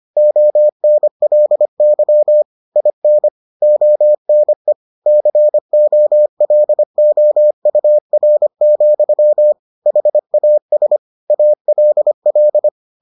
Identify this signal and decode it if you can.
Morse code, heard as a keyed tone, about 25 wpm, 600 Hz. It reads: ONLY IN ONE COLOUR, HAS ALL